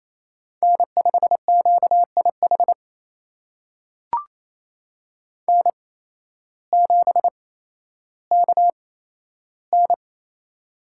Morse code, heard as a keyed tone, about 28 wpm, spaced as for 13 wpm, 700 Hz. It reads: N5QIH N7KN
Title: USA Call Signs - 8x Character Spacing - 28wpm